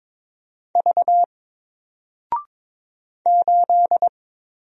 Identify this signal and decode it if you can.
Morse code, heard as a keyed tone, about 22 wpm, 700 Hz. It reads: V 8
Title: Single Character (Letter:Number:Punctuation:Prosign) 22wpm